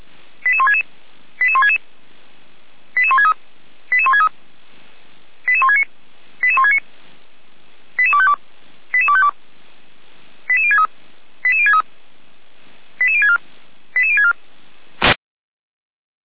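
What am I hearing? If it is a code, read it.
Five-tone selective call (ZVEI), heard as quick runs of five tones, tones of about 70 ms. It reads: 88166 88166 88163 88163 88169 88169 88252 88252 88063 88063 88064 88064
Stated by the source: ZVEI five-tone call received by SDR, logged on the audio server of Feuerwehr Orlamünde (fire brigade).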